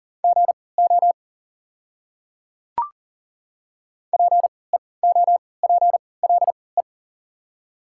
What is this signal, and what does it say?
Morse code, heard as a keyed tone, about 40 wpm, spaced as for 23 wpm, 690 Hz. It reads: GO PEOPLE
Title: 3x Character Spacing - Top 100 Words - 40wpm